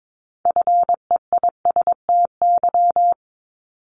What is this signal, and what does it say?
Morse code, heard as a keyed tone, about 22 wpm, 700 Hz. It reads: FEISTY
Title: Multi-letter Phonemes 22wpm